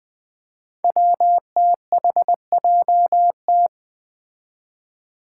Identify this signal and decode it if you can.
Morse code, heard as a keyed tone, about 20 wpm, 700 Hz. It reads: WTHJT
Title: Five Challenging Letter-Number Combinations 20wpm